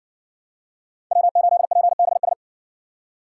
Morse code, heard as a keyed tone, about 60 wpm, 700 Hz. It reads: W8PBS